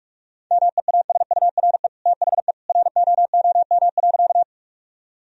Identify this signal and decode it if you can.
Morse code, heard as a keyed tone, about 44 wpm, 700 Hz. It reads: MEASURE THE ROOM.
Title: Sentences from Top 1000 Words - Review Part 2 - 45wpm